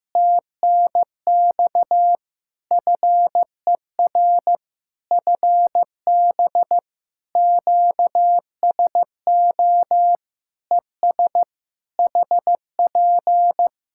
Morse code, heard as a keyed tone, about 15 wpm, 700 Hz. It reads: TNX FER FB QSO ES HP
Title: Practice QSO - Exchange 4 - Ending - 15wpm